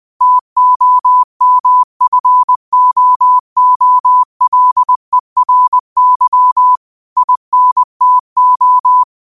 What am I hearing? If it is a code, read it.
Morse code, heard as a keyed tone, about 20 wpm, 1000 Hz. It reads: TOMFOOLERY INTO
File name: qotdfiles / 20wpm